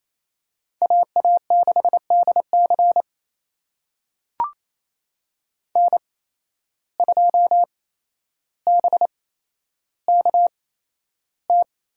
Morse code, heard as a keyed tone, about 28 wpm, spaced as for 12 wpm, 700 Hz. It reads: AA6DC N2BKT
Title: USA Call Signs - 5 Characters - 8x Character Spacing - 28wpm